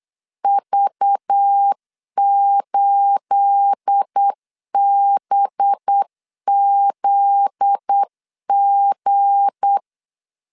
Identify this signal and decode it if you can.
Morse code, heard as a keyed tone, about 8 wpm, 800 Hz. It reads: V8BZG
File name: LEZIONI / 9 28 8wpm 3512